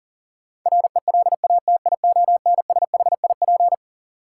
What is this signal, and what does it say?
Morse code, heard as a keyed tone, about 40 wpm, 700 Hz. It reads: RELATIONSHIP
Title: Sentences from Top 1100 Words - New Word Drill Alt - 40wpm